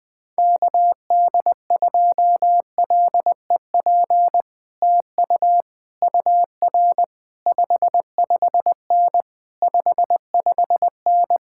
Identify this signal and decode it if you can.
Morse code, heard as a keyed tone, about 20 wpm, 700 Hz. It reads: KD2LEP TU UR 55N 55N
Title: Speed-Racing Common QSO Phrases - 17wpm